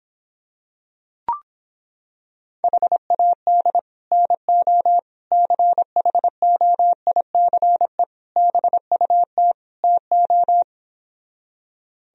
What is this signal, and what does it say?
Morse code, heard as a keyed tone, about 26 wpm, 700 Hz. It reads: HAD NO CHOICE BUT TO